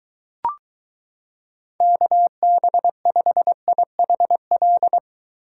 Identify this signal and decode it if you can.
Morse code, heard as a keyed tone, about 22 wpm, 700 Hz. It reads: KB5IHL